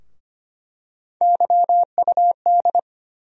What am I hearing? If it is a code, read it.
Morse code, heard as a keyed tone, about 25 wpm, 700 Hz. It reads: YUD
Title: USA Call Sign Postfixes - 3 Letters 25wpm